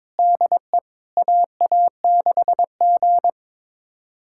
Morse code, heard as a keyed tone, about 23 wpm, 700 Hz. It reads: DE AA6G
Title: USA Callsign DE Callsign - 4 Characters - 22wpm